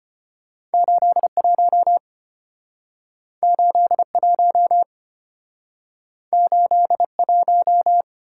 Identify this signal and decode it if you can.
Morse code, heard as a keyed tone, about 34 wpm, 700 Hz. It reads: 81 81 81